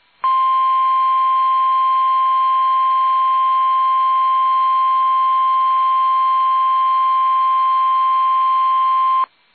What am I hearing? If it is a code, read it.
Emergency Alert System (SAME) tones (the 1050 Hz attention tone). It is heard from 0.2 to 9.3 s.
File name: eas / KUSO / KUSO